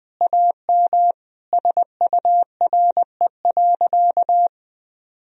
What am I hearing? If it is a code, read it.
Morse code, heard as a keyed tone, about 20 wpm, 700 Hz. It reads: AM SURE.
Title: Sentences from Top 1000 Words - Review Part 3 - 20wpm